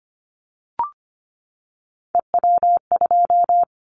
Morse code, heard as a keyed tone, about 25 wpm, 690 Hz. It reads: EW2
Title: Callsign Prefixes for Top 50 DXCC - 25wpm